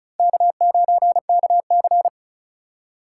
Morse code, heard as a keyed tone, about 45 wpm, 700 Hz. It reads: K9KC